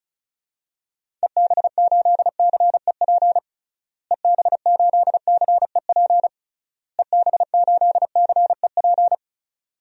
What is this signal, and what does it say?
Morse code, heard as a keyed tone, about 36 wpm, 700 Hz. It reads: EB8CEP EB8CEP EB8CEP
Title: DXCC-100 Call Signs - 6 Characters - Repeated 3x - 35wpm